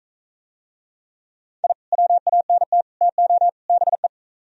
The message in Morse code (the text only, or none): I WANT TO BE